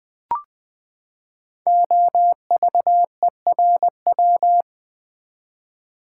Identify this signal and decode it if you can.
Morse code, heard as a keyed tone, about 20 wpm, 700 Hz. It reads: OVERW
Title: Build A Word - 5 Letters At A Time - 12-Letter Words - 20wpm